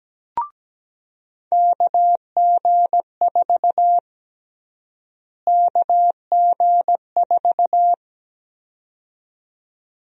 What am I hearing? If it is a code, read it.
Morse code, heard as a keyed tone, about 17 wpm, 700 Hz. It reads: KG4 KG4
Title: USA Call Sign Prefixes - Repeated 2x - 17wpm